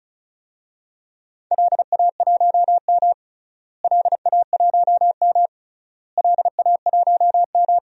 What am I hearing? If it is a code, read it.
Morse code, heard as a keyed tone, about 36 wpm, 700 Hz. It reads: LA1M LA1M LA1M
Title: DXCC-100 Call Signs - 4 Characters - Repeated 3x - 35wpm